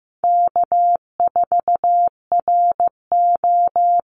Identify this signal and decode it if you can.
Morse code, heard as a keyed tone, about 15 wpm, 700 Hz. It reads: K4RO
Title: Easy Call Signs Repeated 3x - 15wpm